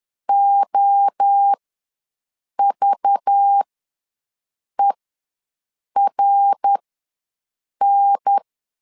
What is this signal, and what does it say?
Morse code, heard as a keyed tone, about 10 wpm, spaced as for 6 wpm, 800 Hz. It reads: OVERN
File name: LEZIONI / 5 LBVC / 5 LBV ws 10wpm 3645